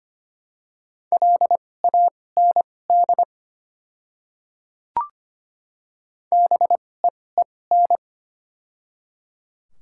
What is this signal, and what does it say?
Morse code, heard as a keyed tone, about 25 wpm, spaced as for 18 wpm, 700 Hz. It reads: LAND BEEN
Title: 2x Character Spacing - Top 200 Words - 25wpm